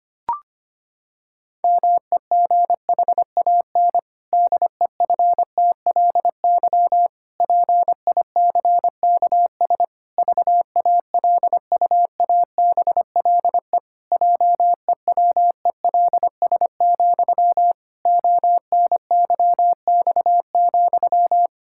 Morse code, heard as a keyed tone, about 25 wpm, 700 Hz. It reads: MEGHAN DEFTLY PICKS VALUABLE JEWELS, ONYX,
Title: Pangram Challenge v2 - 25wpm